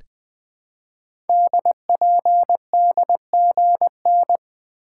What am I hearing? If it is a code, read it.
Morse code, heard as a keyed tone, about 21 wpm, 700 Hz. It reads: DPDGN
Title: Code Groups - 5 Letters - 20wpm